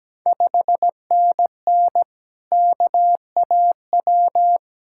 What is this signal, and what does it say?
Morse code, heard as a keyed tone, about 17 wpm, 700 Hz. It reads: 5NN KAW
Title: Ontario QSO Party Exchange 17wpm